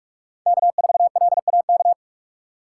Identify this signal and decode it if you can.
Morse code, heard as a keyed tone, about 43 wpm, 700 Hz. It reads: K4LAK
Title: USA Call Signs - 5 Characters - 3x Character Spacing - 45wpm